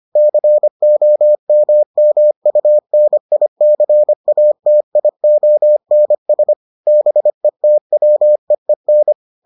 Morse code, heard as a keyed tone, about 25 wpm, 600 Hz. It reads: COMMUNICATIONS BETWEEN